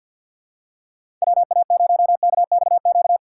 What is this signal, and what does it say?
Morse code, heard as a keyed tone, about 48 wpm, 700 Hz. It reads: WA0KXQ
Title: USA Call Signs - 3x Character Spacing - 50wpm